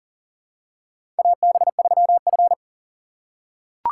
Morse code, heard as a keyed tone, about 40 wpm, 700 Hz. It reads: AB3F
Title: USA Call Signs - 4 Characters - 4x Character Spacing - 40wpm